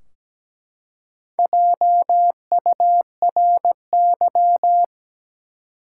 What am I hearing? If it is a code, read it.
Morse code, heard as a keyed tone, about 17 wpm, 700 Hz. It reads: JURY
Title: Four Letter Words - 17wpm